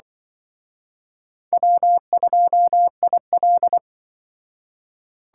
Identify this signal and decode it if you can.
Morse code, heard as a keyed tone, about 23 wpm, 700 Hz. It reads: W2IL
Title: Speed Racing - USA 1x2 Call Signs 20wpm